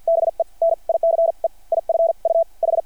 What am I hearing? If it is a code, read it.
Morse code, heard as a keyed tone, about 50 wpm, spaced as for 42 wpm, 660 Hz. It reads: BE NICE IF U H
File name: archived / CRSnet